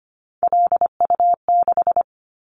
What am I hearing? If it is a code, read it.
Morse code, heard as a keyed tone, about 25 wpm, 700 Hz. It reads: LU6
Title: Callsign Prefixes for Top 100 DXCC - 25wpm